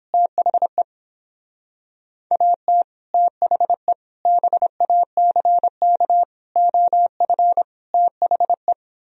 Morse code, heard as a keyed tone, about 29 wpm, 700 Hz. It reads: THE AT THE BACK OF THE